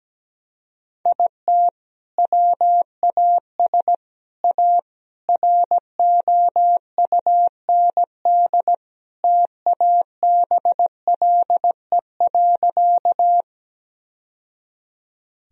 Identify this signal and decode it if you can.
Morse code, heard as a keyed tone, about 17 wpm, 700 Hz. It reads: IT WAS A ROUND TABLE.
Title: Sentences from Top 1000 Words - Review Part 2 - 17wpm